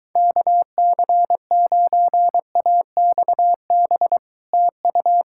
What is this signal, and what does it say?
Morse code, heard as a keyed tone, about 25 wpm, 700 Hz. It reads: KC9AXB TU